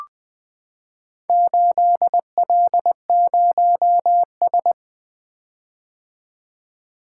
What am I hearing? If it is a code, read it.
Morse code, heard as a keyed tone, about 20 wpm, 700 Hz. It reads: 8L0S
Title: Four Letter-Number Combinations - 20wpm